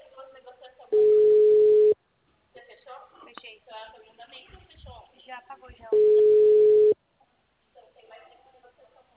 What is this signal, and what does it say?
Ringback tone (Europe, 425 Hz), with speech.
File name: out- -2020- - - .2173